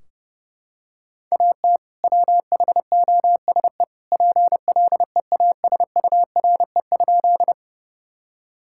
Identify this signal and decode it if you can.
Morse code, heard as a keyed tone, about 32 wpm, 700 Hz. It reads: AT WHOSE PLEASURE?
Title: Sentences from Top 1600 Words 30wpm